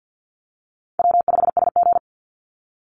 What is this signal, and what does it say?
Morse code, heard as a keyed tone, about 50 wpm, 700 Hz. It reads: W5SG
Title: Easy Call Signs - 50wpm